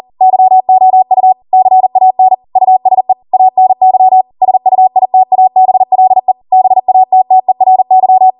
Morse code, heard as a keyed tone, about 40 wpm, 750 Hz. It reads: YOU CAN USE ANY SUITABLE BATTERY
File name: Morse / 40WPM